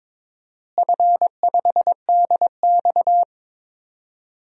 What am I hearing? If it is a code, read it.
Morse code, heard as a keyed tone, about 22 wpm, 700 Hz. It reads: F5DX
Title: DXCC-100 Call Signs - 4 Characters - Repeated 3x - 22wpm